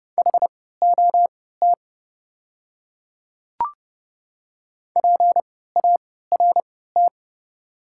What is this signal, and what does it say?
Morse code, heard as a keyed tone, about 30 wpm, spaced as for 17 wpm, 700 Hz. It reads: HOT PART
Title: 3x Character Spacing - Top 200 Words - 30wpm